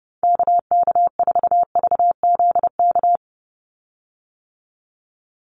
Morse code, heard as a keyed tone, about 30 wpm, 700 Hz. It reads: KK4VZK
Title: USA Call Signs 30wpm